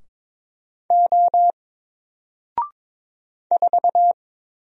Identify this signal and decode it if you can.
Morse code, heard as a keyed tone, about 22 wpm, 700 Hz. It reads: O 4
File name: Single-Letter-Number-22wpm